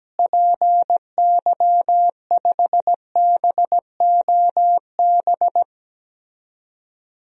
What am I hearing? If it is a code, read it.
Morse code, heard as a keyed tone, about 17 wpm, 700 Hz. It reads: PY5BOB
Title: DXCC-100 Call Signs - 6 Characters - Repeated 2x - 17wpm